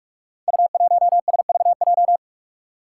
Morse code, heard as a keyed tone, about 44 wpm, 700 Hz. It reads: U1SVJ
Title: Five Challenging Letter-Number Combinations 45wpm